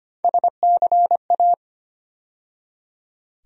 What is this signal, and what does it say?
Morse code, heard as a keyed tone, about 25 wpm, 700 Hz. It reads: SCA
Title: USA Call Sign Postfixes - 3 Letters 25wpm